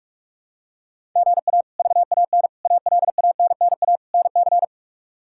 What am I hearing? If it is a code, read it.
Morse code, heard as a keyed tone, about 46 wpm, 700 Hz. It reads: GA VAN ALANNA NC
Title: K1USN SST QSO Practice - 45wpm